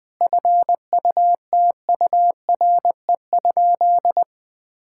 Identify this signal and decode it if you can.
Morse code, heard as a keyed tone, about 20 wpm, 700 Hz. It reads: FUTURE?
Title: Sentences from Top 1000 Words - Review Part 6 - 20wpm